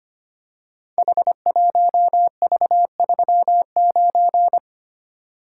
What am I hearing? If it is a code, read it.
Morse code, heard as a keyed tone, about 25 wpm, 700 Hz. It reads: H1V39
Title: Code Groups - 5 Letter:Numbers - 25wpm